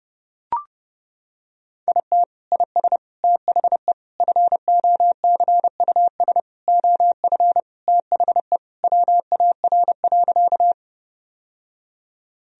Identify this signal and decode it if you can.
Morse code, heard as a keyed tone, about 30 wpm, 700 Hz. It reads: IT IS THE FOCUS OF THE WAR.
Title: Sentences from Top 1000 Words - Review Part 4 - 30wpm